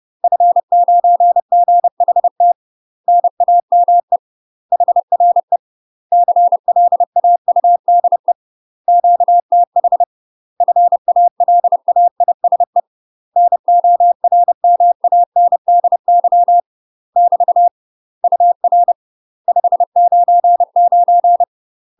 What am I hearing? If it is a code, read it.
Morse code, heard as a keyed tone, about 30 wpm, 700 Hz. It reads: F9GHT NAME HRE CLAUDE QTH FALAISE NORMANDY <BT> UR 599
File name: UK MASTER QSOs TRAINING / 30WPM / UK MASTER QSOs 0021